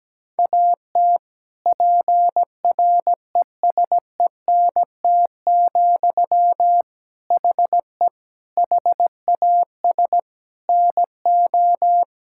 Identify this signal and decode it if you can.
Morse code, heard as a keyed tone, about 17 wpm, 700 Hz. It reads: AT PRESENT, HE HAS NO
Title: Sentences from Top 1000 Words - Review Part 5 - 17wpm